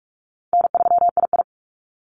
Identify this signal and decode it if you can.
Morse code, heard as a keyed tone, about 44 wpm, 700 Hz. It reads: N3II
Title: NA SOTA Activator Callsigns 45wpm